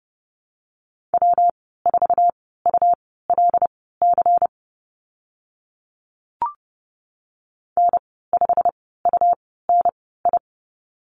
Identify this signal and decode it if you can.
Morse code, heard as a keyed tone, about 30 wpm, spaced as for 17 wpm, 700 Hz. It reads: W4ULC N5UNI
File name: Lesson 19g - Letter w - Call Signs 30wpm